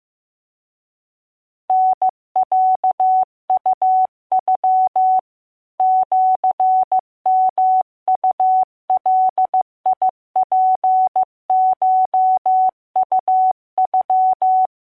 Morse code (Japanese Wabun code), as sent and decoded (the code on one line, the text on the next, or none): -. .-.- ..- ..-- / --.-. -- ..- .-.. .. .--. ---- ..- ..--
タロウノ シヨウガツコウノ